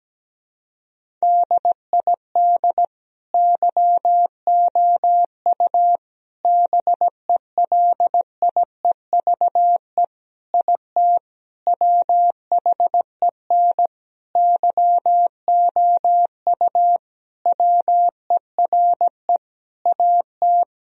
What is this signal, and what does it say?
Morse code, heard as a keyed tone, about 17 wpm, 700 Hz. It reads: DID YOU BELIEVE IT WHEN YOU WERE AT
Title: Sentences from Top 1000 Words - Review Part 1 - 17wpm